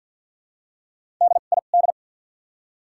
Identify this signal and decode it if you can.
Morse code, heard as a keyed tone, about 53 wpm, spaced as for 37 wpm, 700 Hz. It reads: DID
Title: 2x Character Spacing - Top 100 Words - 50wpm